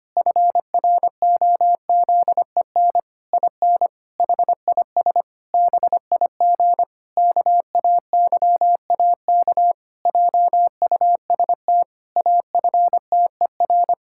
Morse code, heard as a keyed tone, about 25 wpm, 700 Hz. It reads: FROZEN IN HIS BIG KAYAK JUST AFTER
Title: Pangram Challenge v2 - 25wpm